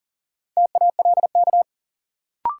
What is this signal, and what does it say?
Morse code, heard as a keyed tone, about 40 wpm, 700 Hz. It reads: TALK
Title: Sentences from Top 400 Words - New Word Drill - 40wpm